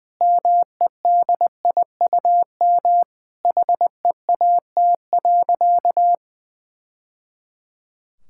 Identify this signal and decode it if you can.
Morse code, heard as a keyed tone, about 20 wpm, 700 Hz. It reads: MEDIUM HEAT.
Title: Sentences from Top 1000 Words - Review Part 6 - 20wpm